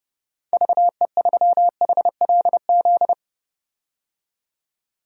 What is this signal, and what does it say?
Morse code, heard as a keyed tone, about 32 wpm, 700 Hz. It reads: VE3HLZ
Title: DXCC-100 Call Signs - 6 Characters - 30wpm